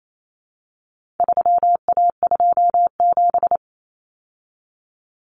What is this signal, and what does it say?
Morse code, heard as a keyed tone, about 28 wpm, 700 Hz. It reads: 3A27
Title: Common Cut Numbers - 4-Digit 28wpm